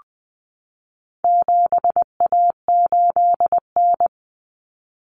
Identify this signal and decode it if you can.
Morse code, heard as a keyed tone, about 20 wpm, 700 Hz. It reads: 7A8N
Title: Common Cut Numbers - 4-Digit 20wpm